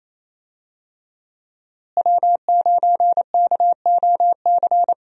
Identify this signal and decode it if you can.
Morse code, heard as a keyed tone, about 28 wpm, 700 Hz. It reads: W9KOC
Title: USA Call Signs - 5 Characters - Repeated 3x - 28wpm